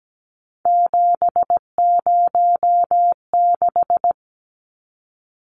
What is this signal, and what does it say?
Morse code, heard as a keyed tone, about 17 wpm, 700 Hz. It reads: 706
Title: Numbers - 3-Digit 17wpm